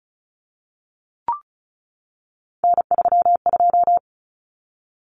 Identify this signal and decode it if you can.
Morse code, heard as a keyed tone, about 34 wpm, 700 Hz. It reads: N32
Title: Common Cut Numbers - 3-Digit 35wpm